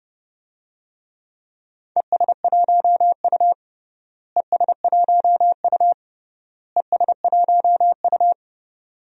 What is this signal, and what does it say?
Morse code, heard as a keyed tone, about 30 wpm, 700 Hz. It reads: ES1U ES1U ES1U
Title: DXCC-100 Call Signs - 4 Characters - Repeated 3x - 30wpm